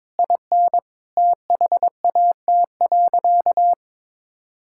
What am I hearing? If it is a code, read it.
Morse code, heard as a keyed tone, about 22 wpm, 700 Hz. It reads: IN THAT.
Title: Sentences from Top 1000 Words - Review Part 5 - 22wpm